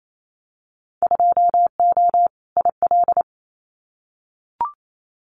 Morse code, heard as a keyed tone, about 28 wpm, 700 Hz. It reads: 2O IL
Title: Winter Field Day Exchanges 28wpm